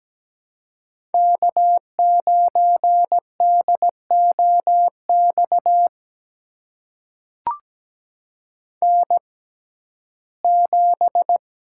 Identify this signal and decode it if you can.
Morse code, heard as a keyed tone, about 17 wpm, spaced as for 12 wpm, 700 Hz. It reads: K9DOX N7
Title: USA Call Signs - 5 Characters - 6x Character Spacing - 17wpm